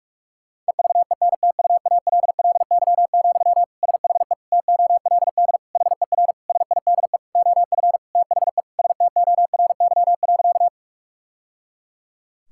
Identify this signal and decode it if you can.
Morse code, heard as a keyed tone, about 46 wpm, 700 Hz. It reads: EVENTUALLY, SHE TOLD HER SIDE OF THE STORY.
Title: Sentences from Top 1000 Words - Review Part 6 - 45wpm